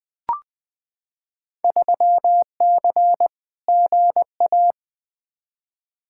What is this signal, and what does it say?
Morse code, heard as a keyed tone, about 20 wpm, 700 Hz. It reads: 3C GA
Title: Field Day Exchanges Encore - 20wpm